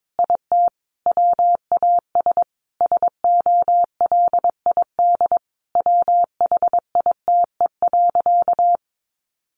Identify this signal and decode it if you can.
Morse code, heard as a keyed tone, about 22 wpm, 700 Hz. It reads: IT WAS SOLID WHITE.
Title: Sentences from Top 1500 Words 22wpm